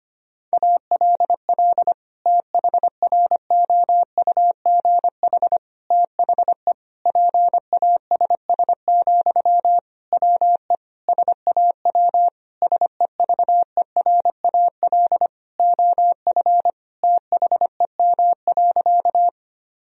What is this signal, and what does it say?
Morse code, heard as a keyed tone, about 25 wpm, 700 Hz. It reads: ALL THROUGH THE PASS, WE SAW SEVERAL OF THEM.
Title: Sentences from Top 1000 Words - Review Part 2 - 25wpm